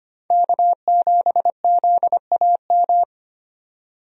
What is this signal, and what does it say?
Morse code, heard as a keyed tone, about 25 wpm, 690 Hz. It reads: K7ZAM
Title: USA Call Signs - 5 Characters - 25wpm